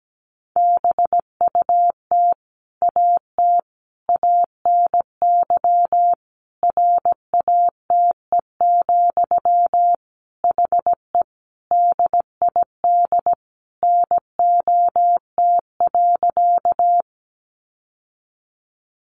Morse code, heard as a keyed tone, about 17 wpm, 700 Hz. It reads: BUT AT ANY RATE, HE DID NOT.
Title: Sentences from Top 700 Words 17wpm